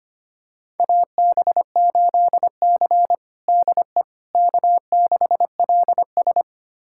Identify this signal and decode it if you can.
Morse code, heard as a keyed tone, about 25 wpm, 700 Hz. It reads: AB8C DE K6LS
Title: USA Callsign DE Callsign - 4 Characters - 25wpm